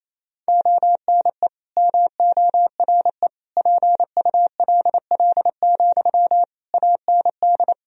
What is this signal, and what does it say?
Morse code, heard as a keyed tone, about 28 wpm, 700 Hz. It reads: ONE MORE PULL, AND
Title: Sentences from Top 1000 Words - Review Part 2 - 28wpm